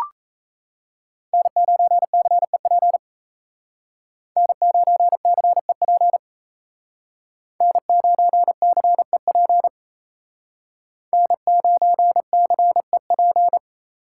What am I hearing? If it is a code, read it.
Morse code, heard as a keyed tone, about 35 wpm, 700 Hz. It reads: N9CEP N9CEP N9CEP N9CEP
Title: Speed Racing - USA 1x3 Call Signs 28wpm